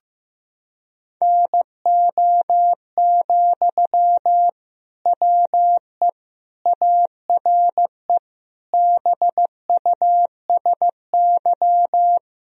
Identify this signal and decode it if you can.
Morse code, heard as a keyed tone, about 15 wpm, 700 Hz. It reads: NO, WE ARE BUSY
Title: Sentences from Top 1000 Words - Review Part 3 - 15wpm